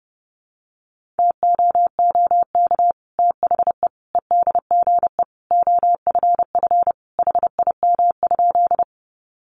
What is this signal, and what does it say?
Morse code, heard as a keyed tone, about 30 wpm, 700 Hz. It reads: TOOK THE EDGE OFF HIM?
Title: Sentences from Top 1300 Words 30wpm